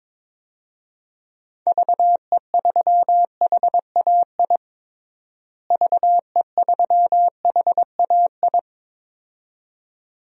Morse code, heard as a keyed tone, about 22 wpm, 700 Hz. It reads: VE3HAI VE3HAI
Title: DXCC-100 Call Signs - 6 Characters - Repeated 2x - 22wpm